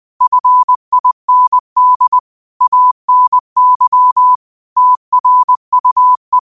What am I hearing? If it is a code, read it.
Morse code, heard as a keyed tone, about 20 wpm, 1000 Hz. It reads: FIND ANY TRUE